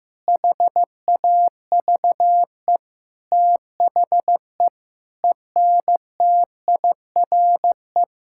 Morse code, heard as a keyed tone, about 15 wpm, 700 Hz. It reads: HAVE THE ENTIRE
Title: Sentences from Top 1000 Words - Review Part 5 - 15wpm